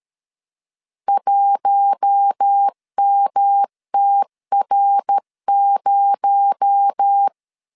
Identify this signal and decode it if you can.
Morse code, heard as a keyed tone, about 13 wpm, 800 Hz. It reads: 1MTR0
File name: LEZIONI / 6 PKX0 / 6 PKX0 12wpm 2231